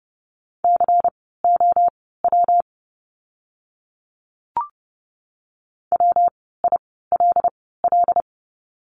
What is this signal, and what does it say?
Morse code, heard as a keyed tone, about 30 wpm, spaced as for 17 wpm, 700 Hz. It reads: COW WILL
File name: course / 30:10wpm / 115 Lesson 19f - Letter w - Words 30wpm